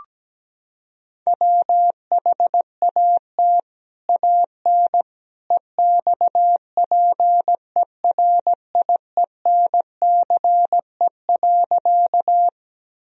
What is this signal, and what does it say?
Morse code, heard as a keyed tone, about 17 wpm, 700 Hz. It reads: WHAT AN EXPERIENCE.
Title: Sentences from Top 1000 Words - Review Part 3 - 17wpm